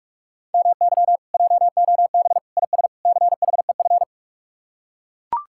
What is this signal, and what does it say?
Morse code, heard as a keyed tone, about 44 wpm, 700 Hz. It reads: MY JOB IS CHEF